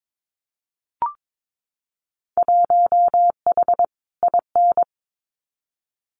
Morse code, heard as a keyed tone, about 22 wpm, 700 Hz. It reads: 1H IN
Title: Winter Field Day Exchanges 22wpm